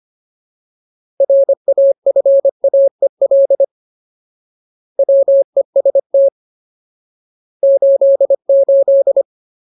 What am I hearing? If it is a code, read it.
Morse code, heard as a keyed tone, about 25 wpm, 550 Hz. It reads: RAFAEL WEST 88
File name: Ultimate Word List / 25WPM Ultimate